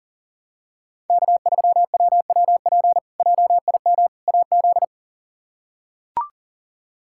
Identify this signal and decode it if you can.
Morse code, heard as a keyed tone, about 40 wpm, 700 Hz. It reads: K3WWP JIM AZ
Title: K1USN SST QSO Practice - 40wpm